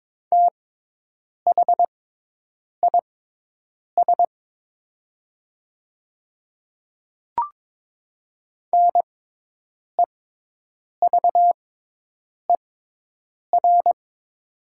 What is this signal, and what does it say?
Morse code, heard as a keyed tone, about 22 wpm, spaced as for 8 wpm, 700 Hz. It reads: THIS NEVER
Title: 6x Character Spacing - Top 300 Words - 22wpm